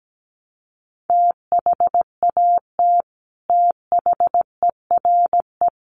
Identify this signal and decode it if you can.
Morse code, heard as a keyed tone, about 17 wpm, 700 Hz. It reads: THAT THERE
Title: Sets of 4 Words 17wpm